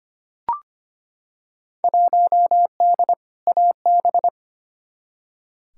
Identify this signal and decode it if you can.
Morse code, heard as a keyed tone, about 25 wpm, 700 Hz. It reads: 1D AB
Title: Field Day Exchanges Encore - 25wpm